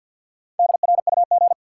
Morse code, heard as a keyed tone, about 51 wpm, 700 Hz. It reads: DRUG
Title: Sets of 5 Words - New Word Drill - 50wpm